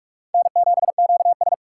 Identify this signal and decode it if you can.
Morse code, heard as a keyed tone, about 56 wpm, 700 Hz. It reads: N7QS
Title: Speed Racing - USA 1x2 Call Signs 45wpm